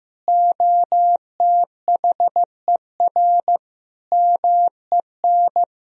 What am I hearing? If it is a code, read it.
Morse code, heard as a keyed tone, about 15 wpm, 700 Hz. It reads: OTHER MEN
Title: Sentences from Top 1000 Words - Review Part 2 - 15wpm